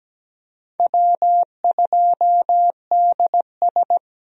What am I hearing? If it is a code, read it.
Morse code, heard as a keyed tone, about 17 wpm, 700 Hz. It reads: W2DS
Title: USA Call Signs - 4 Characters - Repeated 3x - 17wpm